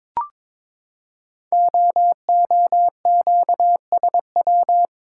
Morse code, heard as a keyed tone, about 22 wpm, 700 Hz. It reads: OOQSW
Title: Code Groups - 5 Letters - 22wpm